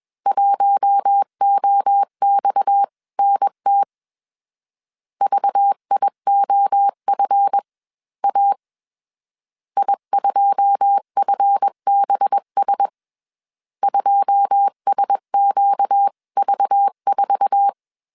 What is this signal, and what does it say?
Morse code, heard as a keyed tone, about 21 wpm, spaced as for 19 wpm, 800 Hz. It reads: 1OXNT VIOFA I2FBS 2SQV4
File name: CW / DET / 1 9 1500C S C 20wpm 2310